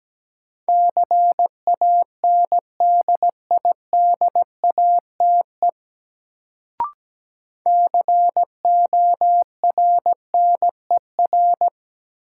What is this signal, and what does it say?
Morse code, heard as a keyed tone, about 17 wpm, 700 Hz. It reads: CANDIDATE CORNER
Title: Sentences from Top 1600 Words - New Word Drill - 17wpm